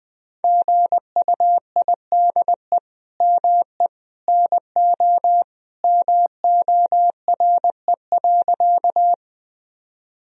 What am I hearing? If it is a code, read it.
Morse code, heard as a keyed tone, about 20 wpm, 700 Hz. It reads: GUIDE ME NO MORE.
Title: Sentences from Top 1000 Words - Review Part 3 - 20wpm